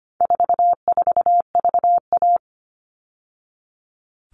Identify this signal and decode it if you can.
Morse code, heard as a keyed tone, about 25 wpm, 700 Hz. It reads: 44VA
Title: All Cut Numbers - 4-Digit 25wpm